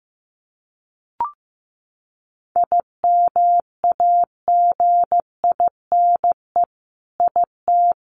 Morse code, heard as a keyed tone, about 15 wpm, 700 Hz. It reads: IMAGINE IT
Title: Sentences from Top 1400 Words 15wpm